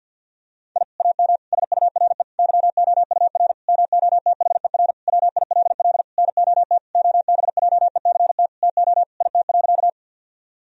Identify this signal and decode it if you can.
Morse code, heard as a keyed tone, about 50 wpm, 700 Hz. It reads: I AM SURE YOUR MOTHER WILL NOT OBJECT TO IT.
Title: Sentences from Top 1000 Words - Review Part 3 - 50wpm